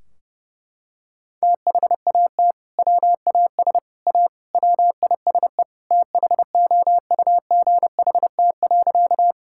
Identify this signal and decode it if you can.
Morse code, heard as a keyed tone, about 30 wpm, 700 Hz. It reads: THAT WAS A WISE THOUGHT.
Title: Sentences from Top 1600 Words 30wpm